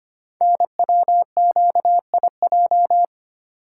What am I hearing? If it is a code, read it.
Morse code, heard as a keyed tone, about 25 wpm, 700 Hz. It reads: NWQIJ
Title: Code Groups - 5 Letter:Numbers - 25wpm